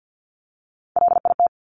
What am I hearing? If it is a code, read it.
Morse code, heard as a keyed tone, about 50 wpm, 700 Hz. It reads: LIT
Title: Top Three Letter Combinations in Words 50wpm